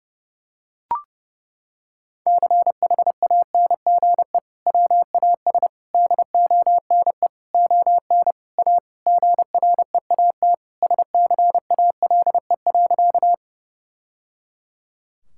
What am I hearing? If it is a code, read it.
Morse code, heard as a keyed tone, about 29 wpm, 700 Hz. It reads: CHANGE WAS DONE ON A GREAT SCALE.
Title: Sentences from Top 1000 Words - Review Part 5 - 30wpm